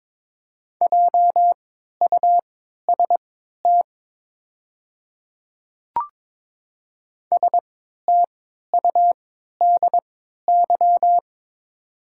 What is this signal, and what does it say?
Morse code, heard as a keyed tone, about 22 wpm, spaced as for 12 wpm, 700 Hz. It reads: JUST STUDY
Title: 3x Character Spacing - Top 300 Words - 22wpm